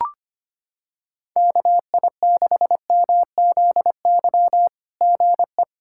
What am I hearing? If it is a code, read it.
Morse code, heard as a keyed tone, about 25 wpm, 700 Hz. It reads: KI6MZY GE
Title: SOTA QSO Practice 25wpm